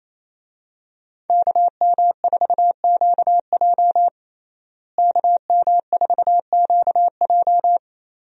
Morse code, heard as a keyed tone, about 28 wpm, 700 Hz. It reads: KM4QJ KM4QJ
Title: USA Call Signs - 5 Characters - Repeated 2x - 28wpm